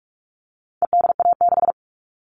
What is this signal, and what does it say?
Morse code, heard as a keyed tone, about 44 wpm, 700 Hz. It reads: EDA6
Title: All Cut Numbers - 4-Digit 45wpm